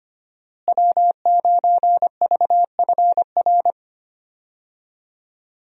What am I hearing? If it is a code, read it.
Morse code, heard as a keyed tone, about 25 wpm, 700 Hz. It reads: W9VFR
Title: USA Call Signs - 5 Characters - Repeated 3x - 25wpm